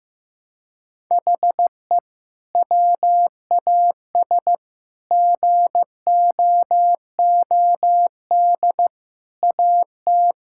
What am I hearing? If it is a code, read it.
Morse code, heard as a keyed tone, about 15 wpm, 700 Hz. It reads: HE WAS GOOD AT
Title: Sentences from Top 1000 Words - Review Part 5 - 15wpm